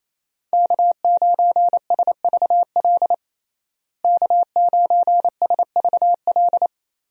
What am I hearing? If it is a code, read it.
Morse code, heard as a keyed tone, about 28 wpm, 700 Hz. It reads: K9SVL K9SVL
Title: USA Call Signs - 5 Characters - Repeated 3x - 28wpm